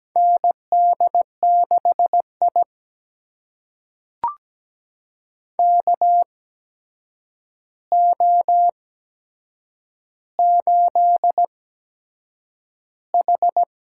Morse code, heard as a keyed tone, about 17 wpm, spaced as for 7 wpm, 700 Hz. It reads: ND6I KO8H
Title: USA Call Signs - 4 Characters - 8x Character Spacing - 17wpm